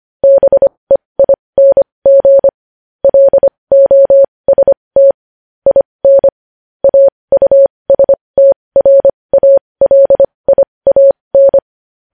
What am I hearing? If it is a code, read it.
Morse code, heard as a keyed tone, about 25 wpm, spaced as for 20 wpm, 550 Hz. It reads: BEING LOST IN AUSTRALIAN
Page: news headlines in morse code at 20 wpm